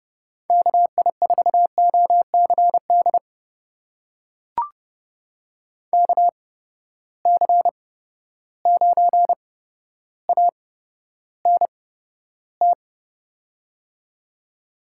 Morse code, heard as a keyed tone, about 31 wpm, spaced as for 13 wpm, 700 Hz. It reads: KI4OCD KC9ANT
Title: USA Call Signs - 6 Characters - 8x Character Spacing - 30wpm